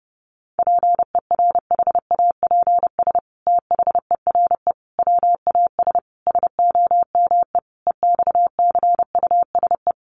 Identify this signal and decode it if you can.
Morse code, heard as a keyed tone, about 30 wpm, 700 Hz. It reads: PERHAPS THERE WAS SOME EXCUSE
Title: Sentences from Top 700 Words 30wpm